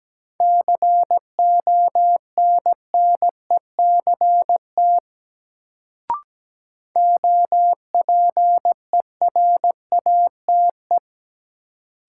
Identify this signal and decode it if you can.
Morse code, heard as a keyed tone, about 17 wpm, 700 Hz. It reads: CONNECT OPERATE
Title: Sentences from Top 1500 Words - New Word Drill - 17wpm